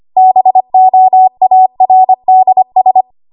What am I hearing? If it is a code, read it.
Morse code, heard as a keyed tone, about 25 wpm, 750 Hz. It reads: BOARDS